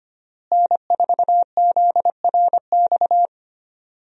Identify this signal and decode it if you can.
Morse code, heard as a keyed tone, about 25 wpm, 700 Hz. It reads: N4ZRX
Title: USA Call Signs - 5 Characters - Repeated 3x - 25wpm